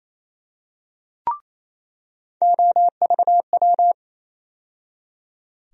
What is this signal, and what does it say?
Morse code, heard as a keyed tone, about 28 wpm, 700 Hz. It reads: OVW
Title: USA Call Sign Postfixes - 3 Letters 28wpm